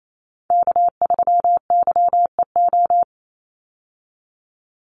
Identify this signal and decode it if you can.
Morse code, heard as a keyed tone, about 28 wpm, 700 Hz. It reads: K3YEO
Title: Call Signs - 28wpm